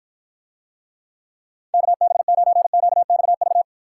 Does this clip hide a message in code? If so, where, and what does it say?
Yes: Morse code from 1.7 to 3.6 s, heard as a keyed tone, about 53 wpm, 700 Hz. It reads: KB9QXV